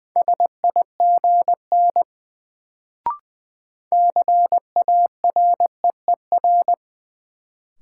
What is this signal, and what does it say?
Morse code, heard as a keyed tone, about 20 wpm, 700 Hz. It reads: SIGN CAREER
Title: Sentences from Top 1000 Words - New Word Drill - 20wpm